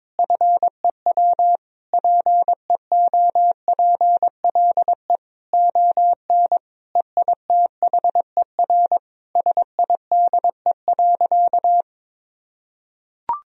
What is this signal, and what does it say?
Morse code, heard as a keyed tone, about 21 wpm, 700 Hz. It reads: FEW PEOPLE ON EITHER SIDE.
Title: Sentences from Top 1000 Words - Review Part 4 - 22wpm